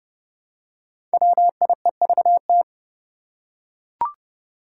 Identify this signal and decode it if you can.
Morse code, heard as a keyed tone, about 30 wpm, 700 Hz. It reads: WIEVT
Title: Five Challenging Letter-Number Combinations 30wpm